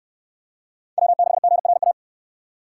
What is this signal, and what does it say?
Morse code, heard as a keyed tone, about 69 wpm, 700 Hz. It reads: W6GRA